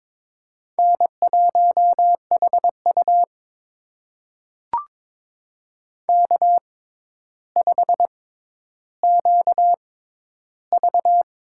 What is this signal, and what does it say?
Morse code, heard as a keyed tone, about 22 wpm, spaced as for 11 wpm, 700 Hz. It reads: N1HU K5QV
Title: USA Call Signs - 4 Characters - 6x Character Spacing - 22wpm